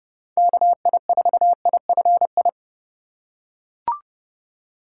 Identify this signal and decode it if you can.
Morse code, heard as a keyed tone, about 30 wpm, 700 Hz. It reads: KI4IFI